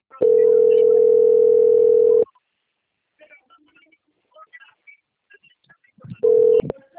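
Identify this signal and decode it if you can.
Ringback tone (North America), 440 + 480 Hz.
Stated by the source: Outbound phone call recorded on a phone system at Sorocaba.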